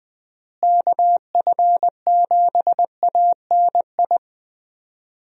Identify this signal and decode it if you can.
Morse code, heard as a keyed tone, about 20 wpm, 700 Hz. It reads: KF7ANI
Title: USA Call Signs - 6 Characters - Repeated 2x - 20wpm